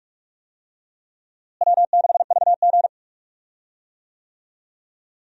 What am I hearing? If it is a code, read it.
Morse code, heard as a keyed tone, about 47 wpm, 700 Hz. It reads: W6VG
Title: USA Call Signs - 4 Characters - Repeated 2x - 45wpm